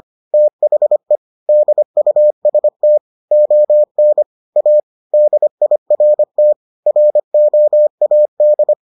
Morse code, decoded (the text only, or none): THE DUST ON A DIRT ROAD